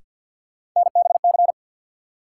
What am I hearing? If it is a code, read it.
Morse code, heard as a keyed tone, about 52 wpm, 700 Hz. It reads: NBC
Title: Sets of 5 Words Encore - New Word Drill - 50wpm